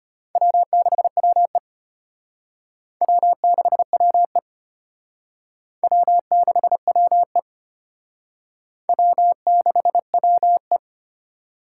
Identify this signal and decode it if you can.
Morse code, heard as a keyed tone, about 31 wpm, 700 Hz. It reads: W6WE W6WE W6WE W6WE
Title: Speed Racing - USA Call Signs 25wpm